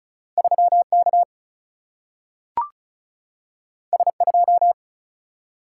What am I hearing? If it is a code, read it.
Morse code, heard as a keyed tone, about 36 wpm, 700 Hz. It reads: 3K S2
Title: Single-Letter Single-Number 35wpm